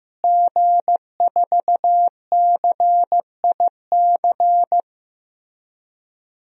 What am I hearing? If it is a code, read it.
Morse code, heard as a keyed tone, about 16 wpm, 700 Hz. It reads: G4CIC
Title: DXCC-100 Call Signs - 5 Characters - Repeated 3x - 15wpm